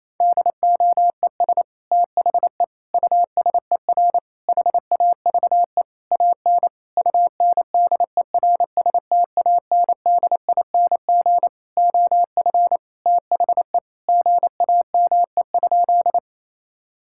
Morse code, heard as a keyed tone, about 28 wpm, 700 Hz. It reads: DOES THE USER HAVE AN UNDERSTANDING OF THE GAME?
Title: Sentences from Top 1000 Words - Review Part 5 - 28wpm